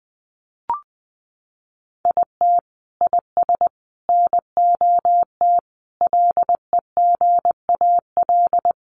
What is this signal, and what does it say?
Morse code, heard as a keyed tone, about 20 wpm, 700 Hz. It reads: IT IS NOT LEGAL
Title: Sentences from Top 1200 Words 20wpm